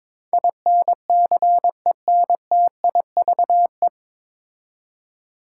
Morse code, heard as a keyed tone, about 22 wpm, 700 Hz. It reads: INCENTIVE
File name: 9 Letter Words - First 3 Letters Given - 22wpm